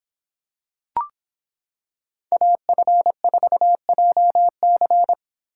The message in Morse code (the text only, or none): AF4JC